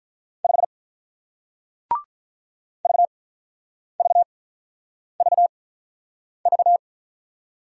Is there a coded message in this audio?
Yes: Morse code from 0.4 to 6.8 s, heard as a keyed tone, about 48 wpm, 700 Hz. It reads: 5 V V V V